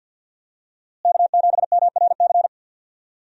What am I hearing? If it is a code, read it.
Morse code, heard as a keyed tone, about 50 wpm, 700 Hz. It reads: K7MRC
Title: USA Call Signs - 5 Characters - Repeated 2x - 50wpm